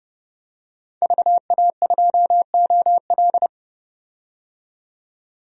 Morse code, heard as a keyed tone, about 32 wpm, 700 Hz. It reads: VA2OL
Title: DXCC-100 Call Signs - 5 Characters - 30wpm